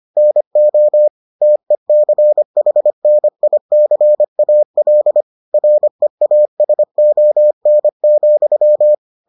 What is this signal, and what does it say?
Morse code, heard as a keyed tone, about 25 wpm, 600 Hz. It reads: NO TECHNICAL REASON,